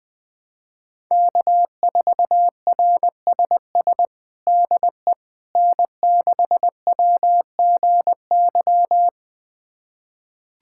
Morse code, heard as a keyed tone, about 21 wpm, 700 Hz. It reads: K4RSS DE N6WGY
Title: USA Callsign DE Callsign 20wpm